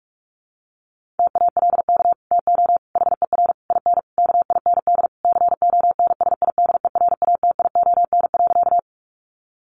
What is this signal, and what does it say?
Morse code, heard as a keyed tone, about 44 wpm, 700 Hz. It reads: TALK TO HER IN KIND CONSIDERATION.
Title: Sentences from Top 1500 Words 45wpm